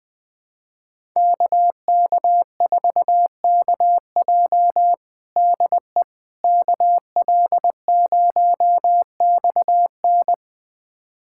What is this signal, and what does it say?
Morse code, heard as a keyed tone, about 20 wpm, 700 Hz. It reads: KK4KJ DE KL0XN
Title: USA Callsign DE Callsign - 5 Characters - 20wpm